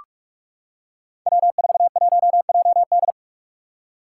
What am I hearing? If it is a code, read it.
Morse code, heard as a keyed tone, about 45 wpm, 700 Hz. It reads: W41JD
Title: Five Challenging Letter-Number Combinations 45wpm